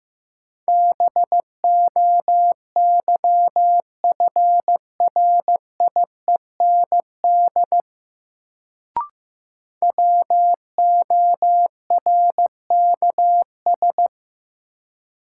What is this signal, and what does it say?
Morse code, heard as a keyed tone, about 15 wpm, 700 Hz. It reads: BOYFRIEND WORKS
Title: Sentences from Top 1000 Words - Review Part 5 - New Word Drill - 15wpm